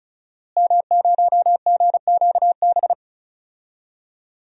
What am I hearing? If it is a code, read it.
Morse code, heard as a keyed tone, about 36 wpm, 700 Hz. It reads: M0GQB
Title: DXCC-100 Call Signs - 5 Characters - Repeated 3x - 35wpm